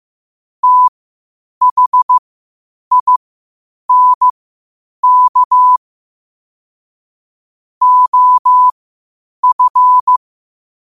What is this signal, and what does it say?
Morse code, heard as a keyed tone, about 15 wpm, spaced as for 8 wpm, 1000 Hz. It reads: THINK OF
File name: qotdfiles / 8wpm